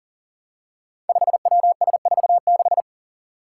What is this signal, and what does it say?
Morse code, heard as a keyed tone, about 40 wpm, 700 Hz. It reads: 5WS46
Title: Five Challenging Letter-Number Combinations 40wpm